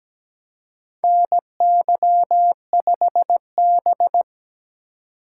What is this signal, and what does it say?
Morse code, heard as a keyed tone, about 17 wpm, 700 Hz. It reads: NY5B
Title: USA Call Signs - 4 Characters - 4x Character Spacing - 17wpm